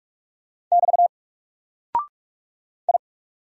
Morse code, heard as a keyed tone, about 44 wpm, 700 Hz. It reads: <BT> I
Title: Single Character (Letter:Number:Punctuation:Prosign) 45wpm